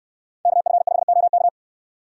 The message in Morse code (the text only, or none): KF5OK